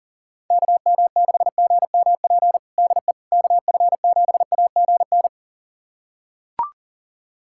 Morse code, heard as a keyed tone, about 40 wpm, 700 Hz. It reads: KM6GMP DE KF7AGN
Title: USA Callsign DE Callsign - 6 Characters - 40wpm